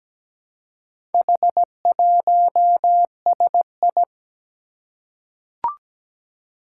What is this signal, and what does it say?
Morse code, heard as a keyed tone, about 17 wpm, 700 Hz. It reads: H1SI
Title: Four Challenging Letter-Number Combinations 17wpm